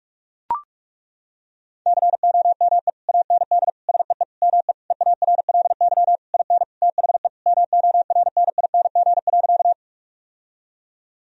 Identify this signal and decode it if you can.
Morse code, heard as a keyed tone, about 42 wpm, 700 Hz. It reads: COME AND SEE ME EARLY IN THE MORNING.
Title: Sentences from Top 1000 Words - Review Part 2 - 45wpm